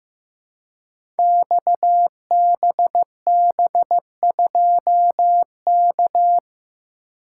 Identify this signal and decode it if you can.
Morse code, heard as a keyed tone, about 15 wpm, 700 Hz. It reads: XBB2K
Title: Code Groups - 5 Letter:Numbers - 15wpm